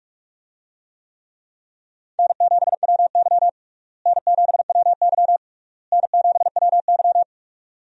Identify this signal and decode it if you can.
Morse code, heard as a keyed tone, about 42 wpm, 700 Hz. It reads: N7WY N7WY N7WY
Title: USA Call Signs - 4 Characters - Repeated 3x - 45wpm